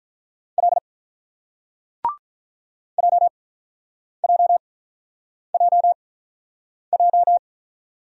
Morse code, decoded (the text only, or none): L JJJJ